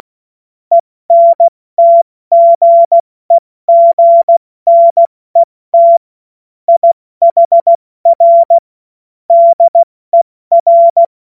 Morse code, heard as a keyed tone, about 16 wpm, 680 Hz. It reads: ENTGEGNET IHR DER
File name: Morsekurs / Witze / Witz 20. Juni 2023 70 F80